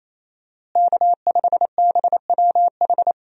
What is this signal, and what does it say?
Morse code, heard as a keyed tone, about 28 wpm, 700 Hz. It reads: K5BWH
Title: USA Call Signs - 5 Characters - Repeated 2x - 28wpm